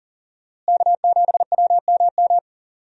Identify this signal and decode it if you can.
Morse code, heard as a keyed tone, about 37 wpm, 700 Hz. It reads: K7WMM
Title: USA Call Signs - 5 Characters - Repeated 3x - 40wpm